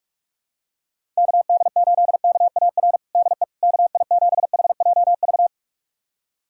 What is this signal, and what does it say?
Morse code, heard as a keyed tone, about 41 wpm, 700 Hz. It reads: KD8KAR DE KI7HJV
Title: USA Callsign DE Callsign - 6 Characters - 45wpm